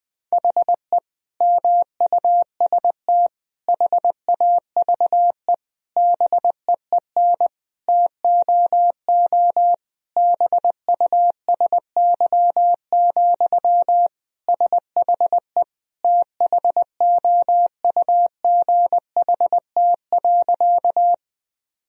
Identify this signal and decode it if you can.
Morse code, heard as a keyed tone, about 20 wpm, 700 Hz. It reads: HE MUST HAVE BEEN TOO BUSY, SHE THOUGHT.
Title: Sentences from Top 1000 Words - Review Part 3 - 20wpm